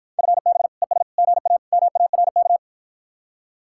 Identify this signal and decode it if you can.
Morse code, heard as a keyed tone, about 53 wpm, 700 Hz. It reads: FB ES GA MARK